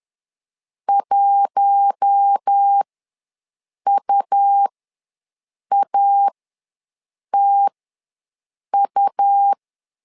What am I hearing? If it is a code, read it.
Morse code, heard as a keyed tone, about 10 wpm, spaced as for 6 wpm, 800 Hz. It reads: 1UATU
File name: LEZIONI / 4 Q1 ws 10wpm 2538